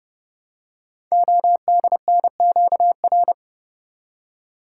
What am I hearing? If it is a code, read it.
Morse code, heard as a keyed tone, about 31 wpm, 700 Hz. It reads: ODNQR
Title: Code Groups - 5 Letters - 30wpm